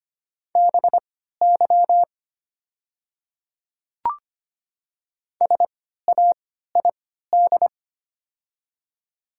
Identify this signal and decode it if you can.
Morse code, heard as a keyed tone, about 25 wpm, spaced as for 14 wpm, 690 Hz. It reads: BY SAID
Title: 3x Character Spacing - Top 100 Words - 25wpm